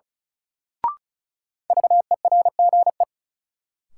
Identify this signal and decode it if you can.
Morse code, heard as a keyed tone, about 36 wpm, 700 Hz. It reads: VERGE
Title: Sets of 4 Words - New Word Drill - 35wpm